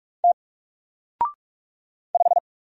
Morse code, heard as a keyed tone, about 44 wpm, 700 Hz. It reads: T 5
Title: Single Character (Letter:Number:Punctuation:Prosign) Alt 45wpm